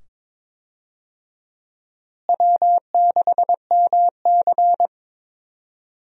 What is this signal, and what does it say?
Morse code, heard as a keyed tone, about 22 wpm, 700 Hz. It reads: W6MC
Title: USA Call Signs - 4 Characters - Repeated 2x - 22wpm